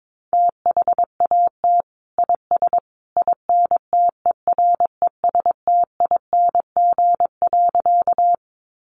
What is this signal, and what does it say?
Morse code, heard as a keyed tone, about 22 wpm, 700 Hz. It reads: THAT IS INTERESTING.
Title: Sentences from Top 1000 Words 22wpm